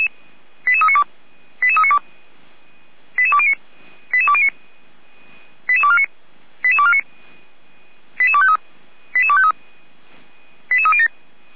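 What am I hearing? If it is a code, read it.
Five-tone selective call (ZVEI), heard as quick runs of five tones, tones of about 80 ms. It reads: 88392 88392 88209 88209 88249 88249 88253 88253 88307
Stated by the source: ZVEI five-tone selective call received on an SDR.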